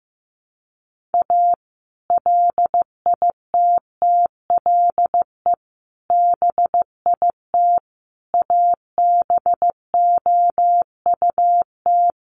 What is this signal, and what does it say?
Morse code, heard as a keyed tone, about 15 wpm, 700 Hz. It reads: A LITTLE BIT ABOUT
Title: Sets of 5 Words Encore2 15wpm